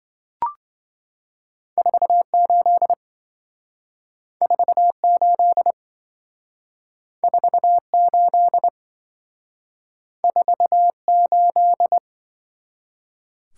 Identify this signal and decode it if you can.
Morse code, heard as a keyed tone, about 27 wpm, 700 Hz. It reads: 48 48 48 48